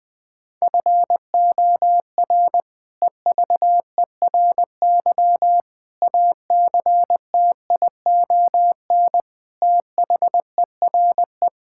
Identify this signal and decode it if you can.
Morse code, heard as a keyed tone, about 20 wpm, 700 Hz. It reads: FOR EVERY ACTION THERE
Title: Sentences with 8-Word Phrases - 20wpm